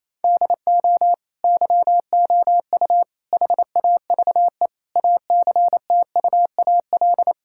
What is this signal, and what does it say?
Morse code, heard as a keyed tone, about 27 wpm, 700 Hz. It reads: DO YOU HAVE ACTUAL
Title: Sentences from Top 1000 Words - Review Part 5 - 28wpm